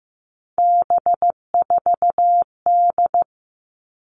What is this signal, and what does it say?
Morse code, heard as a keyed tone, about 15 wpm, 700 Hz. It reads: B4D
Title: All Cut Numbers - 3-Digit 15wpm